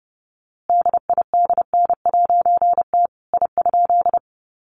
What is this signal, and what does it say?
Morse code, heard as a keyed tone, about 30 wpm, 700 Hz. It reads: DIDN'T I?
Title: A Sound Of Thunder 30wpm